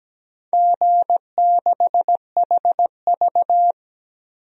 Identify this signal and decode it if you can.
Morse code, heard as a keyed tone, about 17 wpm, 700 Hz. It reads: G6HV
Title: DXCC-100 Call Signs - 4 Characters - 17wpm